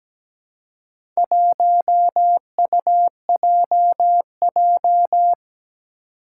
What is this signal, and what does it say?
Morse code, heard as a keyed tone, about 17 wpm, 700 Hz. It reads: 1UJJ
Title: Four Challenging Letter-Number Combinations 17wpm